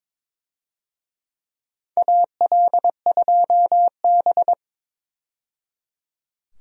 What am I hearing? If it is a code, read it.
Morse code, heard as a keyed tone, about 22 wpm, 700 Hz. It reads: AL2B
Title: USA Call Signs - 4 Characters - 22wpm